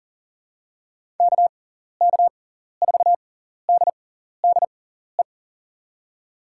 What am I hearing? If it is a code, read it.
Morse code, heard as a keyed tone, about 40 wpm, spaced as for 14 wpm, 700 Hz. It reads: KK4DDE
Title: USA Call Signs - 6x Character Spacing - 40wpm